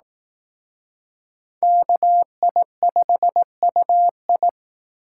Morse code, heard as a keyed tone, about 18 wpm, 700 Hz. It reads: KI5UI
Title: Speed Racing - USA 2x2 Call Signs 15wpm